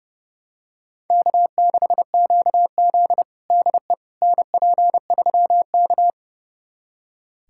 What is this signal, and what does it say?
Morse code, heard as a keyed tone, about 30 wpm, 700 Hz. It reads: K6QZ DE NP3K
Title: USA Callsign DE Callsign - 4 Characters - 30wpm